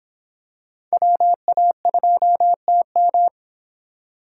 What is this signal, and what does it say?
Morse code, heard as a keyed tone, about 26 wpm, 700 Hz. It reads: WA2TM